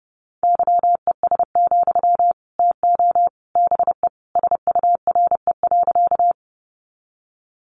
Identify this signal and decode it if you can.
Morse code, heard as a keyed tone, about 30 wpm, 700 Hz. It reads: YES, TO BE SURE.
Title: Sentences from Top 500 Words 30wpm